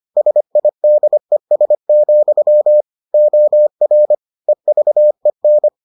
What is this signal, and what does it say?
Morse code, heard as a keyed tone, about 25 wpm, 600 Hz. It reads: SIDES, OR EVEN